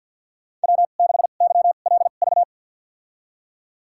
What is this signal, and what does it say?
Morse code, heard as a keyed tone, about 49 wpm, spaced as for 36 wpm, 700 Hz. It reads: W6YLV
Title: USA Call Signs - 5 Characters - 2x Character Spacing - 50wpm